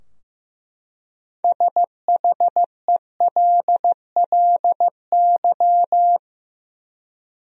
Morse code, heard as a keyed tone, about 15 wpm, 700 Hz. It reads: SHELLY
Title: USA Common Names - 15wpm